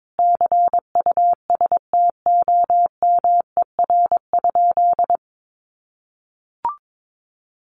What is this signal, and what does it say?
Morse code, heard as a keyed tone, about 22 wpm, 700 Hz. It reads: CUSTOMER?
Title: Sentences from Top 1100 Words 22wpm